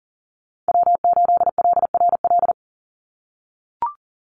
Morse code, heard as a keyed tone, about 40 wpm, 700 Hz. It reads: W8LRL
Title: USA Call Signs 40wpm